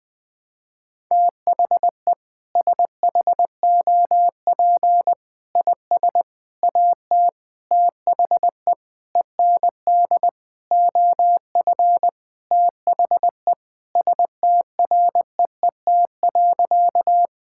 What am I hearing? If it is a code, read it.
Morse code, heard as a keyed tone, about 20 wpm, 700 Hz. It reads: THE SHOP IS AT THE END OF THE STREET.
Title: Sentences with 5-Word Phrases - 20wpm